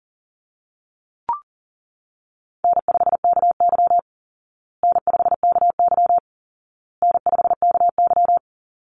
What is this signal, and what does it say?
Morse code, heard as a keyed tone, about 40 wpm, 700 Hz. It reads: N5KY N5KY N5KY
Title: Easy Call Signs Repeated 3x - 40wpm